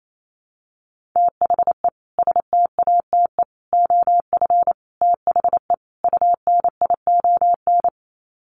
Morse code, heard as a keyed tone, about 28 wpm, 700 Hz. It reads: THE STATE OF THE UNION
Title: Sets of 5 Words Encore 28wpm